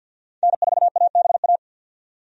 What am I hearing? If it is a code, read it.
Morse code, heard as a keyed tone, about 51 wpm, 700 Hz. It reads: N4ABA
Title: USA Call Signs - 5 Characters - 50wpm